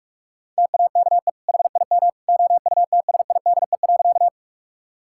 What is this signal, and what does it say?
Morse code, heard as a keyed tone, about 46 wpm, 700 Hz. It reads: TAKE HIM OUTSIDE.
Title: Sentences from Top 1000 Words - Review Part 6 - 45wpm